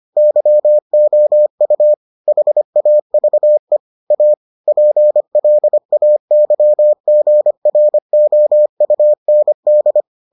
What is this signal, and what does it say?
Morse code, heard as a keyed tone, about 25 wpm, 600 Hz. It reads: YOU HAVE A PLAYGROUND